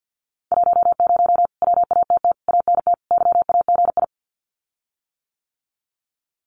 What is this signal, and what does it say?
Morse code, heard as a keyed tone, about 48 wpm, 700 Hz. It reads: 10 WATT ANT YAGI
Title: Common QSO Phrases Encore 50wpm